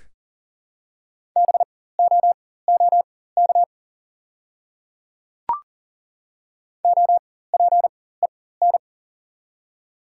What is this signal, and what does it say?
Morse code, heard as a keyed tone, about 40 wpm, spaced as for 19 wpm, 700 Hz. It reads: BOOK OPEN
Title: 4x Character Spacing - Top 300 Words - 40wpm